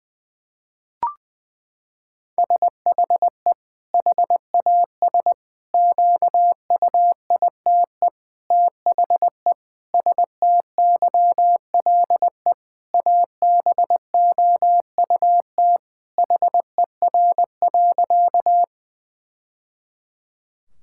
Morse code, heard as a keyed tone, about 20 wpm, 700 Hz. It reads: SHE HAS QUITE THE STYLE ABOUT HER.
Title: Sentences from Top 1000 Words - Review Part 5 - 20wpm